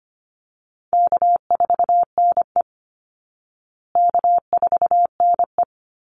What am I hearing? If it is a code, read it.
Morse code, heard as a keyed tone, about 25 wpm, 700 Hz. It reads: K4NE K4NE
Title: Easy Call Signs Repeated 2x - 25wpm